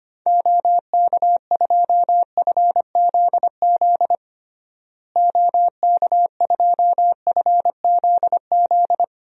Morse code, heard as a keyed tone, about 25 wpm, 700 Hz. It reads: OK2FZZ OK2FZZ
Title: DXCC-100 Call Signs - 6 Characters - Repeated 3x - 25wpm